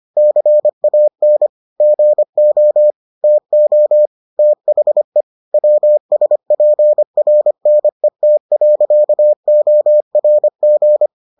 Morse code, heard as a keyed tone, about 25 wpm, 600 Hz. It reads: CAN GO TO THE WSPRNET.ORG